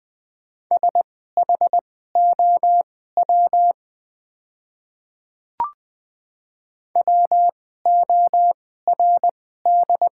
Morse code, heard as a keyed tone, about 20 wpm, spaced as for 14 wpm, 700 Hz. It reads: SHOW WORD
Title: 2x Character Spacing - Top 200 Words - 20wpm